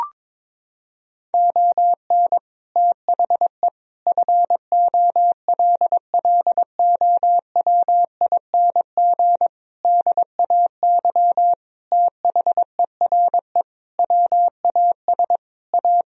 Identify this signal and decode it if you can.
Morse code, heard as a keyed tone, about 22 wpm, 700 Hz. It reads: ON THE FOLLOWING DAY THERE WAS A
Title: Sentences from Top 1000 Words - Review Part 4 - 22wpm